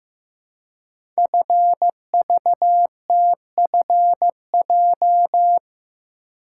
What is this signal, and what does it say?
Morse code, heard as a keyed tone, about 15 wpm, 700 Hz. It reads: FVTFJ
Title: Code Groups - 5 Letters - 15wpm